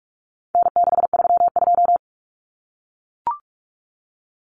Morse code, heard as a keyed tone, about 44 wpm, 700 Hz. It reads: N632
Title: Common Cut Numbers - 4-Digit 45wpm